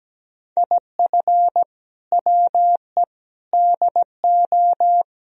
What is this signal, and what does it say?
Morse code, heard as a keyed tone, about 17 wpm, 700 Hz. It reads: IF WE DO
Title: Speed-Racing - Sets of 4 Words 17wpm